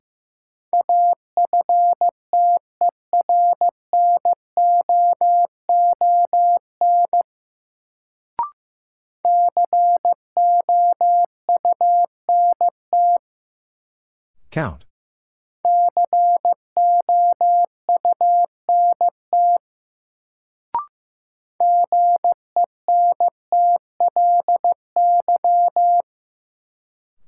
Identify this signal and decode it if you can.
Morse code, heard as a keyed tone, about 15 wpm, 700 Hz. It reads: AFTERNOON COUNT COUNT GENTLY
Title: Sentences from Top 1500 Words - New Word Drill - 15wpm